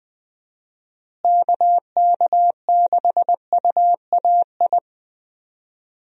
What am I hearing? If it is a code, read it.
Morse code, heard as a keyed tone, about 20 wpm, 700 Hz. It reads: KK6UAI
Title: USA Call Signs - 6 Characters - Repeated 3x - 20wpm